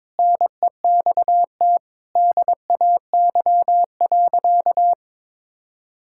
Morse code, heard as a keyed tone, about 22 wpm, 700 Hz. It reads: NEXT DAY.
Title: Sentences from Top 1000 Words - Review Part 6 - 22wpm